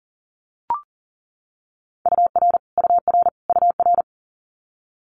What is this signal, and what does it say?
Morse code, heard as a keyed tone, about 42 wpm, 690 Hz. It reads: UR UR UR
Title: Multi-letter Phonemes Repeated 3x 40wpm